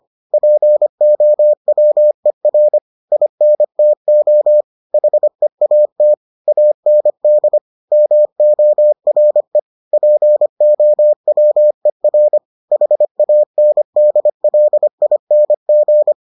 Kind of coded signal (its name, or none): Morse code